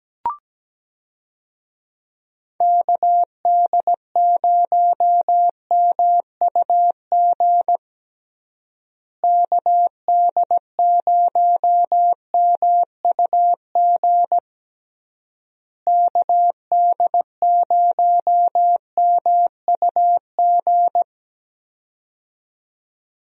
Morse code, heard as a keyed tone, about 17 wpm, 700 Hz. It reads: KD0MUG KD0MUG KD0MUG
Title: USA Call Signs - 6 Characters - Repeated 3x - 17wpm